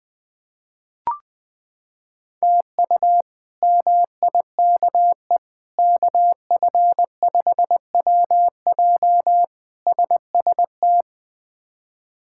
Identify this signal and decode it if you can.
Morse code, heard as a keyed tone, about 21 wpm, 700 Hz. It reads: TU MIKE KF5WJ SST
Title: K1USN SST QSO Practice - 20wpm